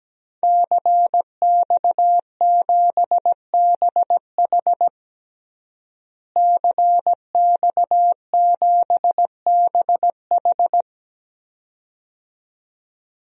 Morse code, heard as a keyed tone, about 17 wpm, 700 Hz. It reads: CX7BH CX7BH
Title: DXCC-100 Call Signs - 5 Characters - Repeated 2x - 17wpm